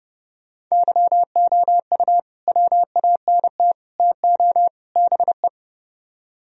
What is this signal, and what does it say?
Morse code, heard as a keyed tone, about 30 wpm, 700 Hz. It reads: YOU WANT TO BE